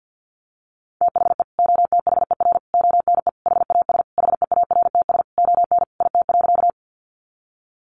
Morse code, heard as a keyed tone, about 50 wpm, 700 Hz. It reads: THE OTHER ONE HAS HEARTS ON IT.
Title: Additional Easy Phrases 50wpm